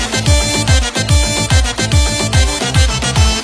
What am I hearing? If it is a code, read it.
There is no code.